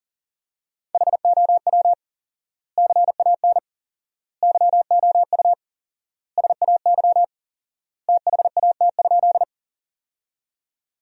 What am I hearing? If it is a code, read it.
Morse code, heard as a keyed tone, about 40 wpm, 700 Hz. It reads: HOW CAN YOU SAY THAT?
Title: 4x Word-Spacing - Sentences from Top 200 Words - 40wpm